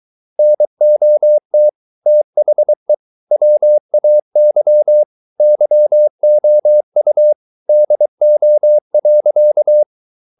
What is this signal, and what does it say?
Morse code, heard as a keyed tone, about 23 wpm, 600 Hz. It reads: NOT THE WAY YOU DO.